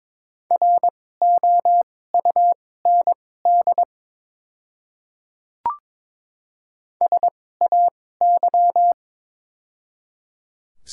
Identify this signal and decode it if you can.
Morse code, heard as a keyed tone, about 22 wpm, spaced as for 16 wpm, 700 Hz. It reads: ROUND SAY
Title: 2x Character Spacing - Top 200 Words - 22wpm